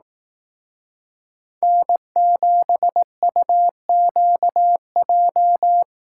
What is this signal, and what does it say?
Morse code, heard as a keyed tone, about 18 wpm, 700 Hz. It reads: N7UQJ
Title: Speed Racing - USA Call Signs 15wpm